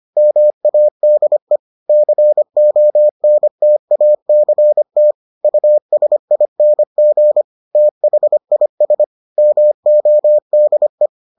Morse code, heard as a keyed tone, about 25 wpm, 600 Hz. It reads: MADE CONTACT USING THIS MODE